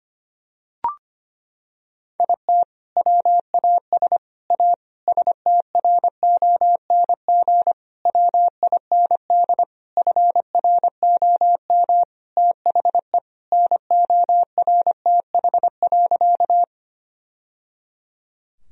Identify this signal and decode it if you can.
Morse code, heard as a keyed tone, about 25 wpm, 700 Hz. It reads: IT WAS A STRONG WIND FROM THE NORTH.
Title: Sentences from Top 1000 Words - Review Part 2 - 25wpm